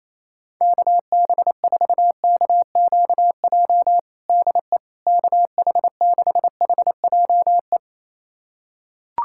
Morse code, heard as a keyed tone, about 28 wpm, 700 Hz. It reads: KB4KQJ DE KH6HJE
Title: USA Callsign DE Callsign 28wpm